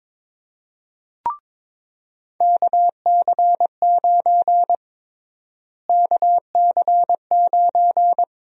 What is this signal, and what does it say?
Morse code, heard as a keyed tone, about 22 wpm, 700 Hz. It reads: KC9 KC9
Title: USA Call Sign Prefixes - Repeated 2x - 22wpm